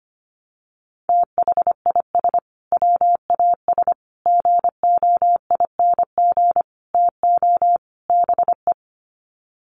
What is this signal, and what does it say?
Morse code, heard as a keyed tone, about 25 wpm, 700 Hz. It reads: THIS WAS GOING TO BE
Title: Sets of 5 Words Encore2 25wpm